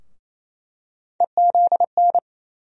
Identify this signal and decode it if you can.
Morse code, heard as a keyed tone, about 28 wpm, 700 Hz. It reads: EZN
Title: USA Call Sign Postfixes - 3 Letters 28wpm